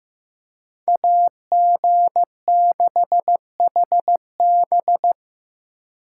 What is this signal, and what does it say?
Morse code, heard as a keyed tone, about 18 wpm, 700 Hz. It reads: AG6HB
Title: Speed Racing - USA 2x2 Call Signs 15wpm